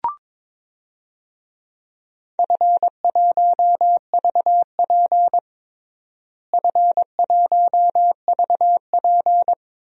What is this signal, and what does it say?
Morse code, heard as a keyed tone, about 22 wpm, 700 Hz. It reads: F1VP F1VP
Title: DXCC-100 Call Signs - 4 Characters - Repeated 2x - 22wpm